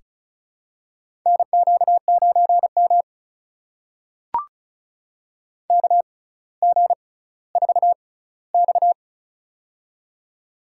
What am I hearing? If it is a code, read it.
Morse code, heard as a keyed tone, about 36 wpm, spaced as for 18 wpm, 700 Hz. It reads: NQ9M KG4X
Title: USA Call Signs - 6x Character Spacing - 35wpm